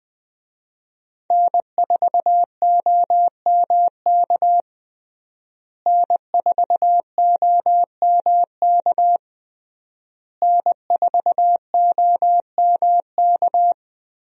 Morse code, heard as a keyed tone, about 20 wpm, 700 Hz. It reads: N4OMK N4OMK N4OMK
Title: USA Call Signs - 5 Characters - Repeated 3x - 20wpm